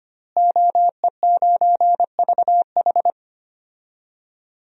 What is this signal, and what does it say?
Morse code, heard as a keyed tone, about 25 wpm, 700 Hz. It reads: OE9VH
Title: DXCC-100 Call Signs - 5 Characters - Repeated 3x - 25wpm